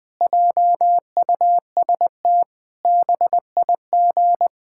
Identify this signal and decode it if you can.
Morse code, heard as a keyed tone, about 20 wpm, 700 Hz. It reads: JUST BIG
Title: Sentences from Top 1000 Words - Review Part 5 - 20wpm